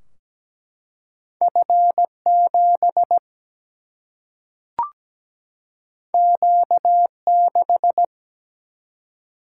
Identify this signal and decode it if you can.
Morse code, heard as a keyed tone, about 17 wpm, 700 Hz. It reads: F7 Q6
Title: Single-Letter Single-Number 17wpm